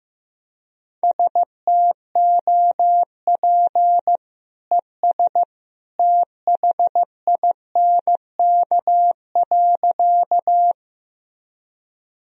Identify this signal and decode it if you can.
Morse code, heard as a keyed tone, about 15 wpm, 700 Hz. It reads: STOP ES THINK.
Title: Sentences from Top 300 Words and Abbreviations 15wpm